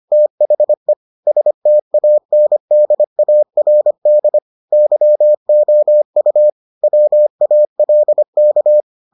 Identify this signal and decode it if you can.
Morse code, heard as a keyed tone, about 25 wpm, 600 Hz. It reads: THE STANDARD YOU WALK